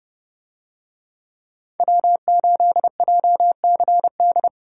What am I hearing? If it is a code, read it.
Morse code, heard as a keyed tone, about 29 wpm, 700 Hz. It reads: W8JCD